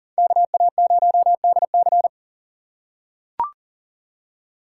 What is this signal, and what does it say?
Morse code, heard as a keyed tone, about 40 wpm, 700 Hz. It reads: KA0DC
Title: USA Call Signs - 5 Characters - 2x Character Spacing - 40wpm